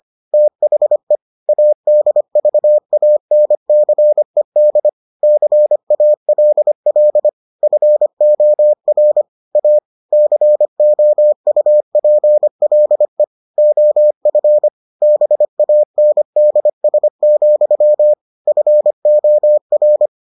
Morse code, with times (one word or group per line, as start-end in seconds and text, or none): 0.3-1.2 THE
1.5-4.9 ADVANCED
5.2-7.3 CALL
7.6-9.2 FOR
9.6-9.8 A
10.1-13.2 COUPLE
13.6-14.7 OF
15.0-18.1 BANDS,
18.5-20.1 FOR